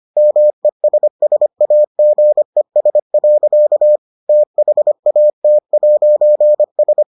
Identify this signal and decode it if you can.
Morse code, heard as a keyed tone, about 25 wpm, 600 Hz. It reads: MESSAGES. THAT'S